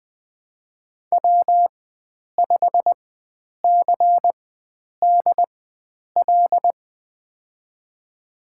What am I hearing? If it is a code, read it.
Morse code, heard as a keyed tone, about 20 wpm, spaced as for 9 wpm, 700 Hz. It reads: W5CDL
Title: USA Call Signs - 5 Characters - 4x Character Spacing - 20wpm